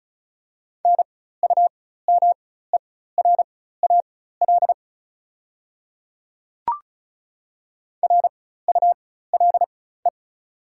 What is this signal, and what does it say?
Morse code, heard as a keyed tone, about 36 wpm, spaced as for 16 wpm, 700 Hz. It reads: NUMERAL RULE
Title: 4x Character Spacing - Top 500 Words - 35wpm